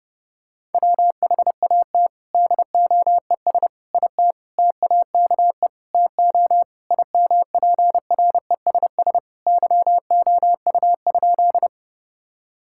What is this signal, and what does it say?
Morse code, heard as a keyed tone, about 30 wpm, 700 Hz. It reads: WHAT DOES IT TAKE TO IMPRESS YOU?
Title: Sentences from Top 1600 Words 30wpm